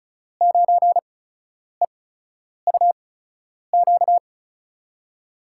Morse code, heard as a keyed tone, about 36 wpm, spaced as for 10 wpm, 700 Hz. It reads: 9EUQ
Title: USA Call Signs - 8x Character Spacing - 35wpm